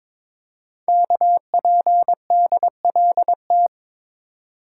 Morse code, heard as a keyed tone, about 21 wpm, 700 Hz. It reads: KPDLT
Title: Code Groups - 5 Letters - 22wpm